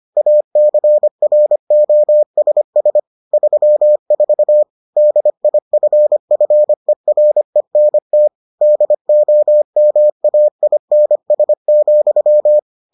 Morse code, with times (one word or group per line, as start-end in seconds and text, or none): 0.2-3.0 ACROSS
3.3-4.6 34
5.0-8.3 DIFFERENT
8.6-12.6 DOMAINS,